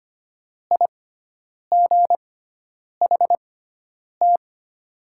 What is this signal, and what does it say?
Morse code, heard as a keyed tone, about 25 wpm, spaced as for 9 wpm, 700 Hz. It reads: IGHT
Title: 6x Character Spacing - Top 500 Words - 25wpm